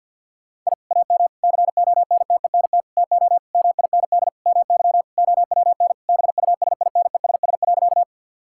Morse code, heard as a keyed tone, about 50 wpm, 700 Hz. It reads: I AM CONTENT TO MIND MY OWN BUSINESS.
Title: Sentences from Top 1000 Words - Review Part 6 - 50wpm